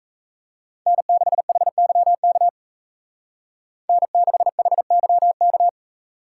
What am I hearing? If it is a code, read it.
Morse code, heard as a keyed tone, about 36 wpm, 700 Hz. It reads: N6HYK N6HYK